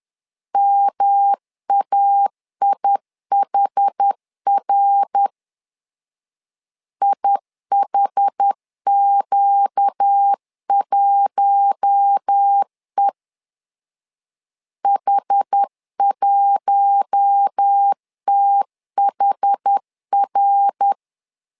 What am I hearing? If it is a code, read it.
Morse code, heard as a keyed tone, about 10 wpm, 800 Hz. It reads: MAIHR IHQ1E H1THR
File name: LEZIONI / 4 Q1 10wpm 2448